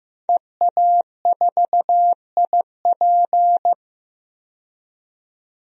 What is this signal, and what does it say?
Morse code, heard as a keyed tone, about 15 wpm, 700 Hz. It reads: EA4IP
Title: DXCC-100 Call Signs - 5 Characters - 15wpm